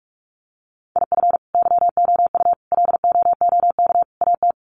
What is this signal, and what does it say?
Morse code, heard as a keyed tone, about 44 wpm, 700 Hz. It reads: IF YOU LOOK AT
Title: Sets of 4 Words 45wpm